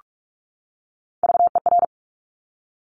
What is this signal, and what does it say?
Morse code, heard as a keyed tone, about 46 wpm, 700 Hz. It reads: VER
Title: Top Three Letter Combinations in Words 45wpm